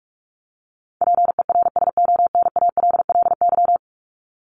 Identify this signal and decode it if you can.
Morse code, heard as a keyed tone, about 44 wpm, 700 Hz. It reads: PERSONALLY
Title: Top 1251-1500 Words 45wpm